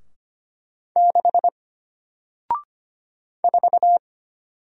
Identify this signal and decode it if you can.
Morse code, heard as a keyed tone, about 25 wpm, 700 Hz. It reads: 6 4
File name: Single-Letter-Number-25wpm